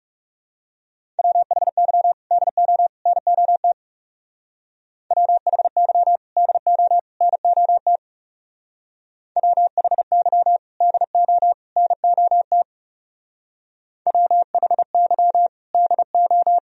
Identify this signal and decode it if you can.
Morse code, heard as a keyed tone, about 37 wpm, 700 Hz. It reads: WHY DO NOT WHY DO NOT WHY DO NOT WHY DO
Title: Speed-Racing - Sets of 3 Words 30wpm